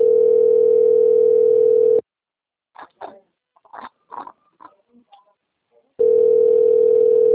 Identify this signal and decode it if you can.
Ringback tone (North America), 440 + 480 Hz.
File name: out- - - - - .29158